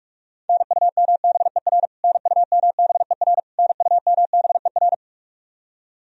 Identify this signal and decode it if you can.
Morse code, heard as a keyed tone, about 42 wpm, 700 Hz. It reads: NUMBER NUMBER NUMBER
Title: Top 100 Words Repeated 45wpm